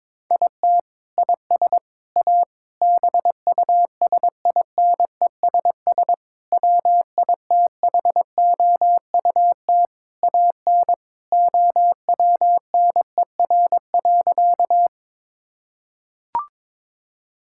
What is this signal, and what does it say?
Morse code, heard as a keyed tone, about 21 wpm, 700 Hz. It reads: IT IS A BUSINESS WITHOUT AN OWNER.
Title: Sentences from Top 1600 Words 22wpm